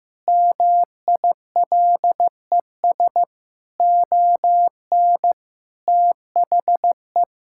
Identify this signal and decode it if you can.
Morse code, heard as a keyed tone, about 15 wpm, 700 Hz. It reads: MILES ON THE
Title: Pangram Challenge v2 - 15wpm